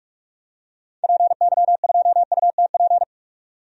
Morse code, heard as a keyed tone, about 47 wpm, 700 Hz. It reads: PY2UTP
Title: DXCC-100 Call Signs - 6 Characters - 45wpm